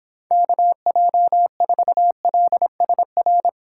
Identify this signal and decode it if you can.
Morse code, heard as a keyed tone, about 26 wpm, 700 Hz. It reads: KJ4LSR